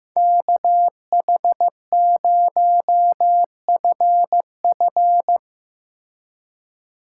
Morse code, heard as a keyed tone, about 17 wpm, 700 Hz. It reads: KH0FF
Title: Speed Racing - USA 2x2 Call Signs 15wpm